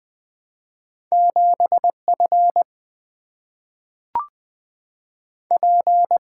Morse code, heard as a keyed tone, about 20 wpm, 700 Hz. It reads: 7 F P
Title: Single-Letter Single-Number 20wpm